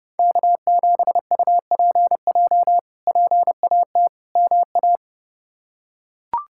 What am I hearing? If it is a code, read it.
Morse code, heard as a keyed tone, about 32 wpm, 700 Hz. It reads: K7UPJ PAT MA
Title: K1USN SST QSO Practice - 30wpm